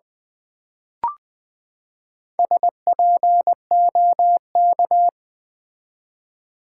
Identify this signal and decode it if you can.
Morse code, heard as a keyed tone, about 20 wpm, 700 Hz. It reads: SPOK
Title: Build A Word - 4 Letters At A Time - 12-Letter Words - 20wpm